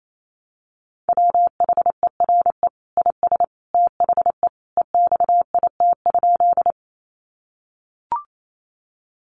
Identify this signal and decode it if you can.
Morse code, heard as a keyed tone, about 28 wpm, 700 Hz. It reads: WHERE IS THE EXIT?
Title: Sentences from Top 1100 Words 28wpm